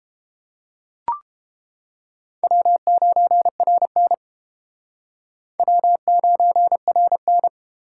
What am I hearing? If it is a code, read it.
Morse code, heard as a keyed tone, about 31 wpm, 700 Hz. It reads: W9RN W9RN